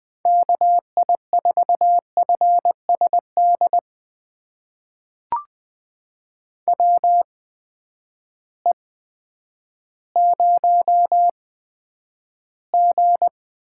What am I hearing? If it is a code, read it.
Morse code, heard as a keyed tone, about 20 wpm, spaced as for 10 wpm, 700 Hz. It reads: KI4FSD WE0G
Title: USA Call Signs - 8x Character Spacing - 20wpm